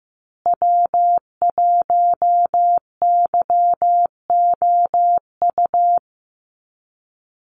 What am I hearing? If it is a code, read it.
Morse code, heard as a keyed tone, about 15 wpm, 700 Hz. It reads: W1YOU
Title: USA Call Signs Repeated 3x 15wpm